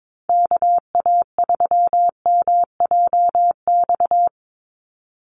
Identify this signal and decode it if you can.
Morse code, heard as a keyed tone, about 22 wpm, 700 Hz. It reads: KA3MJX
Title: USA Call Signs 22wpm